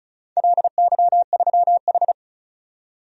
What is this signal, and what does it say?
Morse code, heard as a keyed tone, about 36 wpm, 700 Hz. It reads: LY3H
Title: DXCC-100 Call Signs - 4 Characters - 35wpm